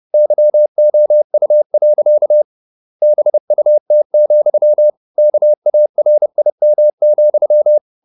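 Morse code, heard as a keyed tone, about 30 wpm, 600 Hz. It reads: YOU. BUT, KARIM,
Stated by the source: Morse code at 30 wpm, the novel Never by Ken Follett in English, from CW med Gnister.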